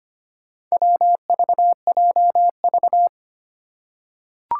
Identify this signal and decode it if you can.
Morse code, heard as a keyed tone, about 25 wpm, 700 Hz. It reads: WVJV
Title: Four Challenging Letter-Number Combinations 25wpm